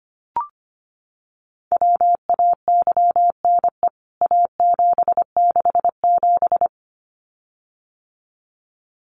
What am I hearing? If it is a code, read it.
Morse code, heard as a keyed tone, about 25 wpm, 700 Hz. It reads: WAYNE A767
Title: CWT Exchange 25wpm